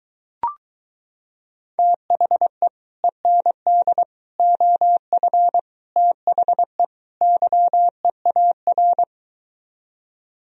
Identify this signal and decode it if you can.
Morse code, heard as a keyed tone, about 23 wpm, 700 Hz. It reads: THE END OF THE YEAR